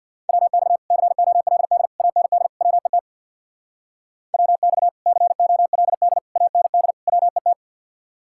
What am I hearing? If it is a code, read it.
Morse code, heard as a keyed tone, about 64 wpm, 700 Hz. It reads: WX COLD AND WET WX COLD AND WET